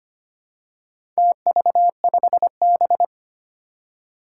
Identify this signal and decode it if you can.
Morse code, heard as a keyed tone, about 25 wpm, 700 Hz. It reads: TV5B
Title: Four Challenging Letter-Number Combinations 25wpm